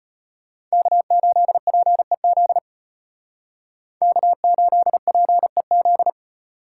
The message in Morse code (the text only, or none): K8PEZ K8PEZ